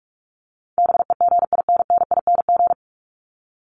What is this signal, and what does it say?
Morse code, heard as a keyed tone, about 44 wpm, 700 Hz. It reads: BEGINNING
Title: ING Suffix - Top 100 Words with Base Words 45wpm